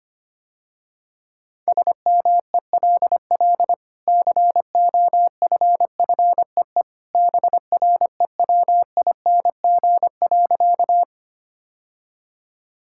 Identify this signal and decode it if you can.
Morse code, heard as a keyed tone, about 25 wpm, 700 Hz. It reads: SMELL COFFEE BREWING.
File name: Easy Sentences with Hints - 25wpm